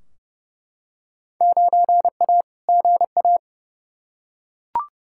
Morse code, heard as a keyed tone, about 30 wpm, 700 Hz. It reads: 9A GA
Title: Field Day Exchanges Encore - 30wpm